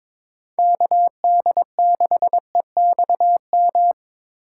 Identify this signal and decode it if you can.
Morse code, heard as a keyed tone, about 22 wpm, 700 Hz. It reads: KD6EXM
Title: USA Call Signs - 6 Characters - 4x Character Spacing - 22wpm